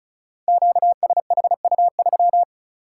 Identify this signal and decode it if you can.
Morse code, heard as a keyed tone, about 36 wpm, 700 Hz. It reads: QSHU3
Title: Code Groups - 5 Letter:Numbers - 35wpm